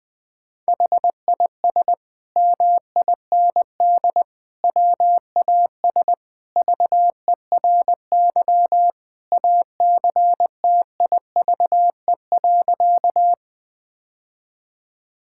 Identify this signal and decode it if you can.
Morse code, heard as a keyed tone, about 20 wpm, 700 Hz. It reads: HIS MIND WAS VERY ACTIVE.
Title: Sentences from Top 1000 Words - Review Part 6 - 20wpm